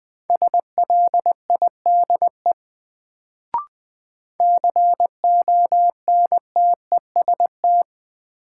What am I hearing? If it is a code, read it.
Morse code, heard as a keyed tone, about 20 wpm, 700 Hz. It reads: SLIDE CONTEST
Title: Sentences from Top 1600 Words - New Word Drill - 20wpm